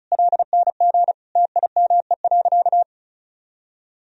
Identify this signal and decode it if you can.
Morse code, heard as a keyed tone, about 34 wpm, 700 Hz. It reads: LNG TIME.
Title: Sentences from Top 400 Words and Abbreviations 35wpm